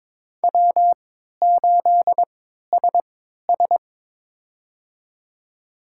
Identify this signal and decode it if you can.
Morse code, heard as a keyed tone, about 22 wpm, spaced as for 13 wpm, 700 Hz. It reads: W8SS
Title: USA Call Signs - 3x Character Spacing - 22wpm